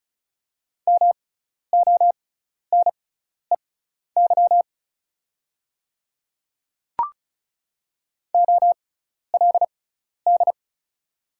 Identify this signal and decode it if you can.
Morse code, heard as a keyed tone, about 34 wpm, spaced as for 12 wpm, 700 Hz. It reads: MONEY OLD
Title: 6x Character Spacing - Top 500 Words - 35wpm